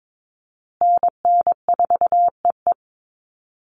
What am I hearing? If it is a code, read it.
Morse code, heard as a keyed tone, about 22 wpm, 700 Hz. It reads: NN4EE
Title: USA Call Signs Repeated 2x 22wpm